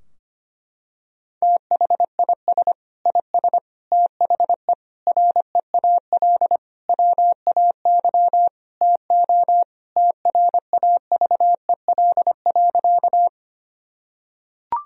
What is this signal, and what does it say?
Morse code, heard as a keyed tone, about 25 wpm, 700 Hz. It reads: THIS IS THE REAL WAY TO TRAVEL.
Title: Sentences from Top 1000 Words - Review Part 2 - 25wpm